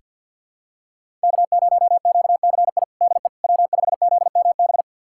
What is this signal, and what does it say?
Morse code, heard as a keyed tone, about 49 wpm, 700 Hz. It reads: K0QCI DE W5ZMB
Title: USA Callsign DE Callsign - 5 Characters - 50wpm